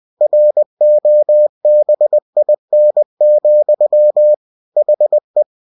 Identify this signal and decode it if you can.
Morse code, heard as a keyed tone, about 20 wpm, 600 Hz. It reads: ROBIN, HE
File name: cw-books / winnie-the-pooh / winnie-20wpm-page-0004